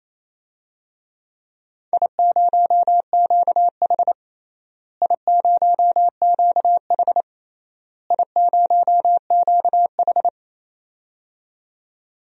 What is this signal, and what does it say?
Morse code, heard as a keyed tone, about 28 wpm, 700 Hz. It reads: I0QH I0QH I0QH
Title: DXCC-100 Call Signs - 4 Characters - Repeated 3x - 28wpm